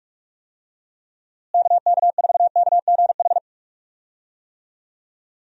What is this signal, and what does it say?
Morse code, heard as a keyed tone, about 41 wpm, 700 Hz. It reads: KK4KGH
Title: USA Call Signs - 6 Characters - Repeated 2x - 45wpm